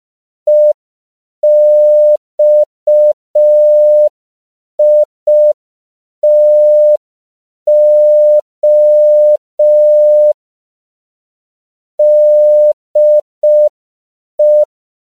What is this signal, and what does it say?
Morse code, heard as a keyed tone, about 5 wpm, 600 Hz. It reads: EXITO DE
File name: CW / 5PPM